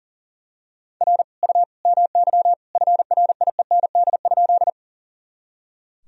Morse code, heard as a keyed tone, about 37 wpm, 700 Hz. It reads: R U MY FRIEND?
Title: Sentences from Top 300 Words and Abbreviations 40wpm